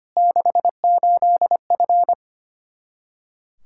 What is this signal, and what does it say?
Morse code, heard as a keyed tone, about 25 wpm, 700 Hz. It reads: 68F
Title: Practice QSO - Exchange 2 - Info - 25wpm